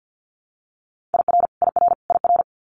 Morse code, heard as a keyed tone, about 50 wpm, 700 Hz. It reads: IR IR IR
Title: Multi-letter Phonemes Repeated 3x 50wpm